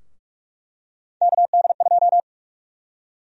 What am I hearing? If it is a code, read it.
Morse code, heard as a keyed tone, about 44 wpm, 700 Hz. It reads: KD2
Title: USA Call Sign Prefixes - Repeated 2x - 45wpm